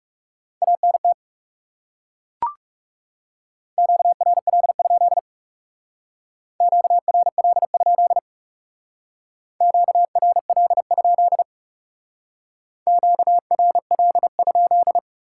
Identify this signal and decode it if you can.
Morse code, heard as a keyed tone, about 40 wpm, 700 Hz. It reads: ANT QRL? QRL? QRL? QRL?